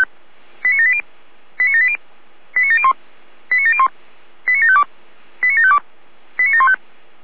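Five-tone selective call (ZVEI), heard as quick runs of five tones, tones of about 70 ms. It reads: 89780 89780 89701 89701 89752 89752 89715
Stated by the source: ZVEI five-tone selective call received on an SDR.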